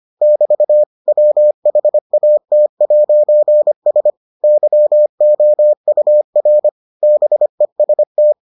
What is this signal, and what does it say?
Morse code, heard as a keyed tone, about 25 wpm, 600 Hz. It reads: <BT> WHAT'S YOUR BEST